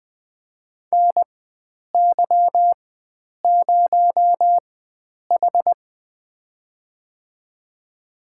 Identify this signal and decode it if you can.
Morse code, heard as a keyed tone, about 20 wpm, spaced as for 9 wpm, 700 Hz. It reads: NY0H
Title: USA Call Signs - 4 Characters - 4x Character Spacing - 20wpm